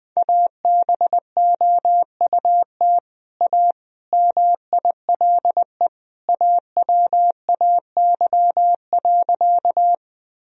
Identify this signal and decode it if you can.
Morse code, heard as a keyed tone, about 20 wpm, 700 Hz. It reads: ABOUT A MILE AWAY.
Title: Sentences from Top 1000 Words - Review Part 1 - 20wpm